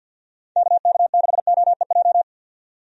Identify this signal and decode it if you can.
Morse code, heard as a keyed tone, about 48 wpm, 700 Hz. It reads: KK6OEJ
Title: USA Call Signs - 6 Characters - 50wpm